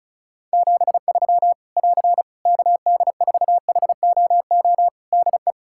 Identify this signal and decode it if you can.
Morse code, heard as a keyed tone, about 36 wpm, 700 Hz. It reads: 73 <AR> KD4HOO DE
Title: Practice QSO - Exchange 4 - Ending - 35wpm